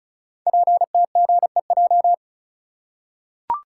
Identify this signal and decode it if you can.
Morse code, heard as a keyed tone, about 36 wpm, 700 Hz. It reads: PTGEJ
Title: Code Groups - 5 Letters - 35wpm